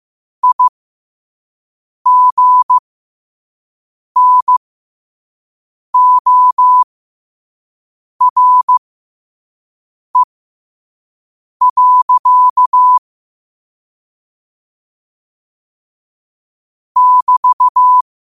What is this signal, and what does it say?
Morse code, heard as a keyed tone, about 15 wpm, spaced as for 5 wpm, 1000 Hz. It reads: IGNORE. <BT>
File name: qotdfiles / 5wpm